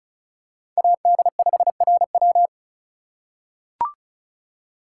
Morse code, heard as a keyed tone, about 36 wpm, 700 Hz. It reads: AD5RW
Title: USA Call Signs - 5 Characters - 2x Character Spacing - 35wpm